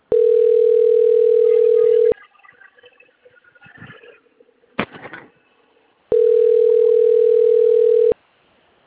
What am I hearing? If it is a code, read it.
Ringback tone (North America), 440 + 480 Hz.
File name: out- - - - - .5404